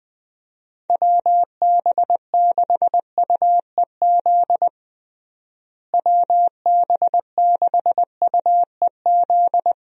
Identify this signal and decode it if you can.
Morse code, heard as a keyed tone, about 20 wpm, 700 Hz. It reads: WB6UEZ WB6UEZ
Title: USA Call Signs - 6 Characters - Repeated 2x - 20wpm